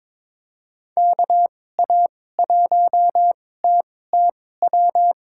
Morse code, heard as a keyed tone, about 22 wpm, spaced as for 16 wpm, 700 Hz. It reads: KA1TTW
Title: USA Call Signs - 6 Characters - 2x Character Spacing - 22wpm